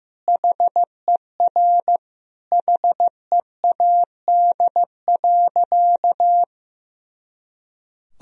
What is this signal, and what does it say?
Morse code, heard as a keyed tone, about 16 wpm, 700 Hz. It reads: HER HEAD.
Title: Sentences from Top 300 Words and Abbreviations 15wpm